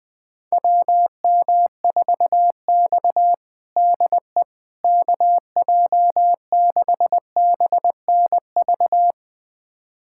Morse code, heard as a keyed tone, about 21 wpm, 700 Hz. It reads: WM4X DE KJ6BNV
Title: USA Callsign DE Callsign 20wpm